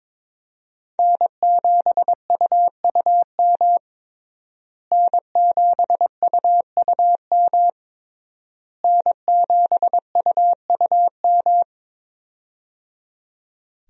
Morse code, heard as a keyed tone, about 22 wpm, 700 Hz. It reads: N7UUM N7UUM N7UUM
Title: USA Call Signs - 5 Characters - Repeated 3x - 22wpm